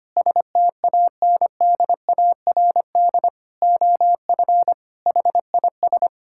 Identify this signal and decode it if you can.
Morse code, heard as a keyed tone, about 25 wpm, 700 Hz. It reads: STANDARD OF HIS
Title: Sentences from Top 1000 Words - Review Part 6 - 25wpm